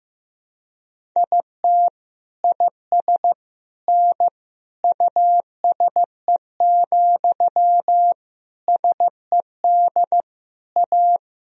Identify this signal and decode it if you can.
Morse code, heard as a keyed tone, about 15 wpm, 700 Hz. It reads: IT IS N USE, SED A
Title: Sentences from Top 500 Words and Abbreviations 15wpm